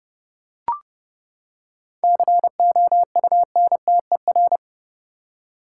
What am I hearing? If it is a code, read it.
Morse code, heard as a keyed tone, about 29 wpm, 700 Hz. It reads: COUNTER
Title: Seven Letter Words - 30wpm